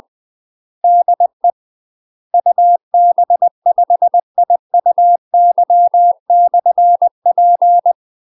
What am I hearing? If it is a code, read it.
Morse code, heard as a keyed tone, about 20 wpm, 700 Hz. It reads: DE UB5IUY/P
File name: UK MASTER QSOs TRAINING / 20WPM / UK MASTER QSOs 0010